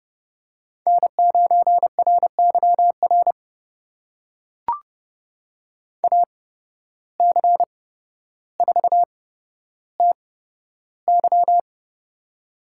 Morse code, heard as a keyed tone, about 30 wpm, spaced as for 13 wpm, 700 Hz. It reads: N9RYR AC4TY
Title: USA Call Signs - 5 Characters - 8x Character Spacing - 30wpm